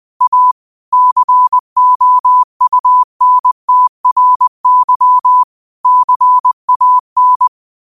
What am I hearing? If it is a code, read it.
Morse code, heard as a keyed tone, about 20 wpm, 1000 Hz. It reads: A COUNTRY CAN